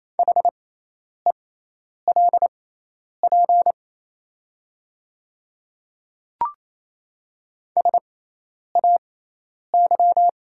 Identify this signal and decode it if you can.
Morse code, heard as a keyed tone, about 27 wpm, spaced as for 10 wpm, 700 Hz. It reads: HELP SAY
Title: 6x Character Spacing - Top 200 Words - 28wpm